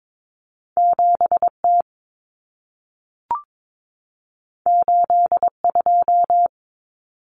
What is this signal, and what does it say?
Morse code, heard as a keyed tone, about 22 wpm, 700 Hz. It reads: 7T 82
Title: Common Cut Numbers - 2-Digit 22wpm